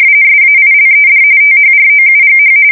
RTTY, about 45.45 baud. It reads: SAILED THROUGH
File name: fsk170 / USS-New-Jersey